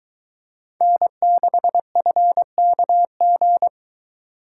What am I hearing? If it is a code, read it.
Morse code, heard as a keyed tone, about 22 wpm, 700 Hz. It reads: N6FKG